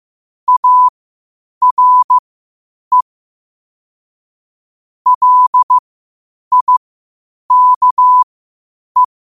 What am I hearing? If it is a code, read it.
Morse code, heard as a keyed tone, about 15 wpm, spaced as for 8 wpm, 1000 Hz. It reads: ARE LIKE
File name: qotdfiles / 8wpm